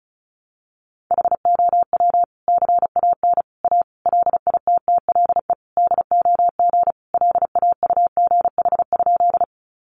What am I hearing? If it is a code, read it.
Morse code, heard as a keyed tone, about 34 wpm, 700 Hz. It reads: HOW CAN A LITTLE DOG LAUGH?
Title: Sentences from Top 500 Words 35wpm